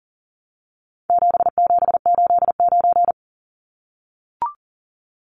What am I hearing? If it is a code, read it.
Morse code, heard as a keyed tone, about 40 wpm, 700 Hz. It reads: 7789
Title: Numbers - 4-Digit 40wpm